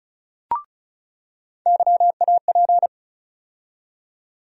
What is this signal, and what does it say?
Morse code, heard as a keyed tone, about 34 wpm, 700 Hz. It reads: YAP
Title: USA Call Sign Postfixes - 3 Letters 35wpm